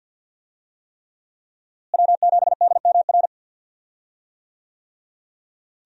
Morse code, heard as a keyed tone, about 46 wpm, 700 Hz. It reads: W7DMR
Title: USA Call Signs - 5 Characters - Repeated 3x - 50wpm